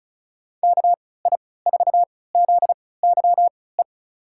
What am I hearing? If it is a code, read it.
Morse code, heard as a keyed tone, about 36 wpm, spaced as for 20 wpm, 700 Hz. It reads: KI4ZYE
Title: USA Call Signs - 3x Character Spacing - 35wpm